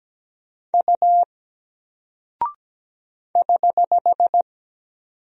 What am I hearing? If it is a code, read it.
Morse code, heard as a keyed tone, about 17 wpm, 700 Hz. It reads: U <HH>
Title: Single Character (Letter:Number:Punctuation:Prosign) Alt 17wpm